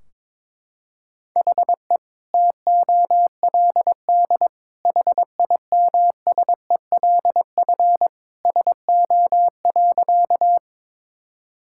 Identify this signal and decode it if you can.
Morse code, heard as a keyed tone, about 22 wpm, 700 Hz. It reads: HE TOLD HIMSELF SO.
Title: Sentences from Top 1600 Words 22wpm